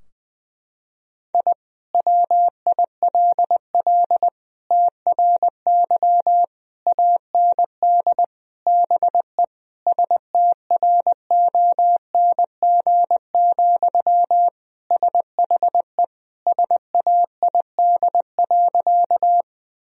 Morse code, heard as a keyed tone, about 20 wpm, 700 Hz. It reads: I WILL TRY AND BE STRONG, SHE SAID.
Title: Sentences from Top 1000 Words - Review Part 3 - 20wpm